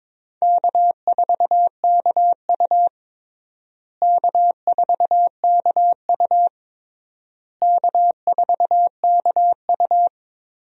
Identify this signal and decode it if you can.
Morse code, heard as a keyed tone, about 22 wpm, 700 Hz. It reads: K4KU K4KU K4KU
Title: USA Call Signs - 4 Characters - Repeated 3x - 22wpm